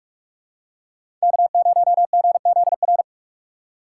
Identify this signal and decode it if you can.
Morse code, heard as a keyed tone, about 43 wpm, 690 Hz. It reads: K0GZR
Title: USA Call Signs - 4x Character Spacing - 45wpm